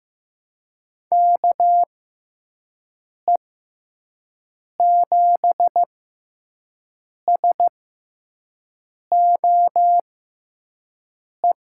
Morse code, heard as a keyed tone, about 15 wpm, spaced as for 5 wpm, 700 Hz. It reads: KE7SOE
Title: USA Call Signs - 6x Character Spacing - 15wpm